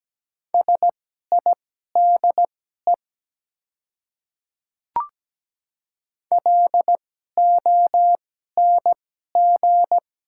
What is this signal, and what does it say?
Morse code, heard as a keyed tone, about 17 wpm, spaced as for 12 wpm, 700 Hz. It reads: SIDE LONG
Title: 2x Character Spacing - Top 100 Words - 17wpm